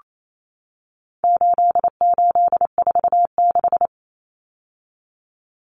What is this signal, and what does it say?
Morse code, heard as a keyed tone, about 28 wpm, 700 Hz. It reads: 8846
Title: Numbers - 4-Digit 28wpm